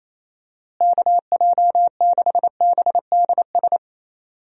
Morse code, heard as a keyed tone, about 28 wpm, 700 Hz. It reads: KJ6BDS
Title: USA Call Signs - 6 Characters - Repeated 2x - 28wpm